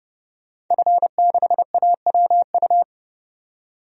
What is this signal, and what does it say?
Morse code, heard as a keyed tone, about 30 wpm, 700 Hz. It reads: F6AWU
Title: DXCC-100 Call Signs - 5 Characters - 30wpm